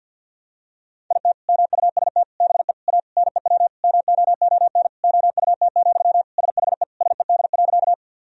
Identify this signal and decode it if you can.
Morse code, heard as a keyed tone, about 52 wpm, 700 Hz. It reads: IT MUST BE A NEW MOON OUT, SHE SED.
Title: Sentences from Top 500 Words and Abbreviations 50wpm